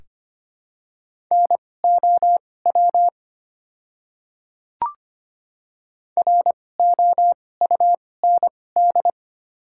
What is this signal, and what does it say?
Morse code, heard as a keyed tone, about 25 wpm, spaced as for 18 wpm, 700 Hz. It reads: NOW ROUND
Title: 2x Character Spacing - Top 200 Words - 25wpm